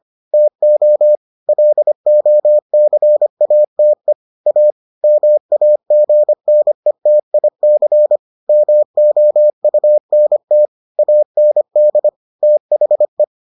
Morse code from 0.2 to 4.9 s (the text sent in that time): TO LOCATE A